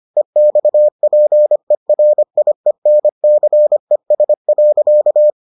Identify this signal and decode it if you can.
Morse code, heard as a keyed tone, about 25 wpm, 600 Hz. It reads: EXPERIENCES.